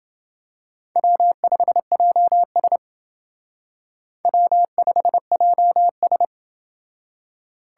Morse code, heard as a keyed tone, about 28 wpm, 700 Hz. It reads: W5JS W5JS